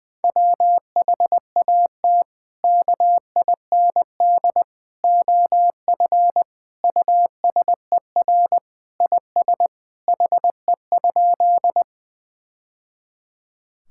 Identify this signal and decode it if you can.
Morse code, heard as a keyed tone, about 20 wpm, 700 Hz. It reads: WHAT KIND OF USER IS HE?
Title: Sentences from Top 1000 Words - Review Part 5 - 20wpm